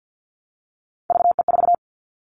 Morse code, heard as a keyed tone, about 50 wpm, 700 Hz. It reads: VE4
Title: Callsign Prefixes for Top 25 DXCC - 50wpm